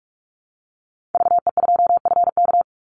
Morse code, heard as a keyed tone, about 46 wpm, 700 Hz. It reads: VE2FK
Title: Call Signs - 45wpm